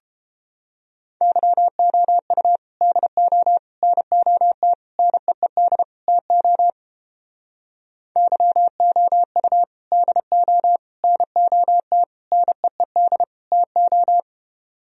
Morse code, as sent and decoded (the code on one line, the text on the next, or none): -.-- --- ..- / -.. --- / -. --- - / -. . -.. / - --- / -.-- --- ..- / -.. --- / -. --- - / -. . -.. / - ---
YOU DO NOT NEED TO YOU DO NOT NEED TO